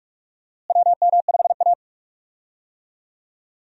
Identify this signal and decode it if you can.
Morse code, heard as a keyed tone, about 41 wpm, 700 Hz. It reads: WM5A
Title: USA Call Signs - 4 Characters - Repeated 3x - 45wpm